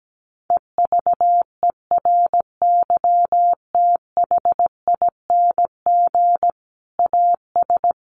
Morse code, heard as a keyed tone, about 17 wpm, 700 Hz. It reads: EVERYTHING AS
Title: Sentences from Top 700 Words 17wpm